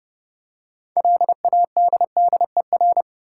Morse code, heard as a keyed tone, about 30 wpm, 700 Hz. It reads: LADDER
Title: Six Letter Words - 30wpm